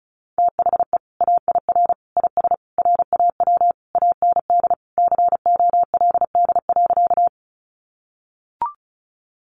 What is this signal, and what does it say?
Morse code, heard as a keyed tone, about 34 wpm, 700 Hz. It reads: THE AIR IS RAW AND COLD.
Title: Sentences from Top 1500 Words 35wpm